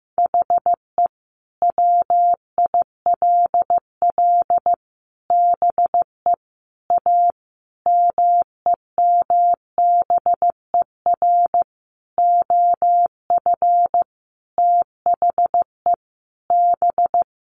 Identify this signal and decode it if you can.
Morse code, heard as a keyed tone, about 15 wpm, 700 Hz. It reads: HE WILL BE A MEMBER OF THE B
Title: Sentences from Top 1100 Words 15wpm